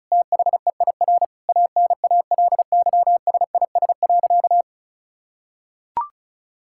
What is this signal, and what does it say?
Morse code, heard as a keyed tone, about 36 wpm, 700 Hz. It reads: THEIR ANALYSIS.
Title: Sentences from Top 1000 Words - Review Part 5 - 35wpm